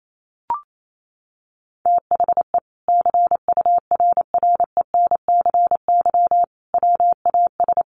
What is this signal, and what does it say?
Morse code, heard as a keyed tone, about 28 wpm, 700 Hz. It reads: THE CURRENCY WAS
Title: Sentences from Top 1500 Words 28wpm